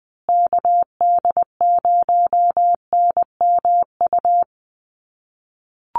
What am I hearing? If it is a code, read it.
Morse code, heard as a keyed tone, about 20 wpm, 700 Hz. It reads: KD0NMU
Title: USA Call Signs Repeated 2x 20wpm